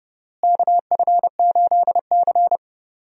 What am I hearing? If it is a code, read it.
Morse code, heard as a keyed tone, about 32 wpm, 700 Hz. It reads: KF8C
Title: USA Call Signs - 4 Characters - Repeated 2x - 30wpm